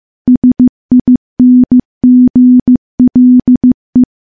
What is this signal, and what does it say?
Morse code, heard as a keyed tone, about 15 wpm, 250 Hz. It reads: SINGLE
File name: morse / morse-training-15WPM